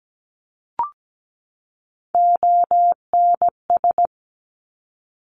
Morse code, heard as a keyed tone, about 17 wpm, 700 Hz. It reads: ONS
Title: Top Three Letter Combinations in Words 17wpm